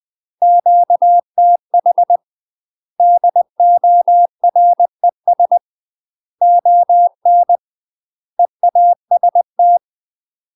Morse code, heard as a keyed tone, about 20 wpm, 700 Hz. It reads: QTH DORES ON EAST
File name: UK MASTER QSOs TRAINING / 20WPM / UK MASTER QSOs 0025